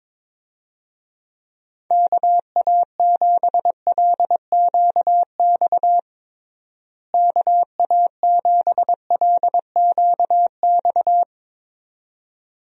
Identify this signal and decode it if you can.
Morse code, heard as a keyed tone, about 22 wpm, 700 Hz. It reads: KA7LQX KA7LQX
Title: USA Call Signs - 6 Characters - Repeated 2x - 22wpm